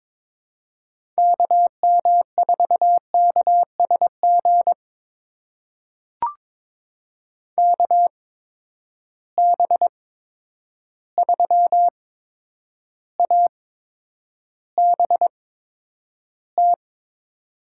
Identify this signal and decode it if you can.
Morse code, heard as a keyed tone, about 23 wpm, spaced as for 9 wpm, 700 Hz. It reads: KM4KSG KB3ABT
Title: USA Call Signs - 6 Characters - 8x Character Spacing - 22wpm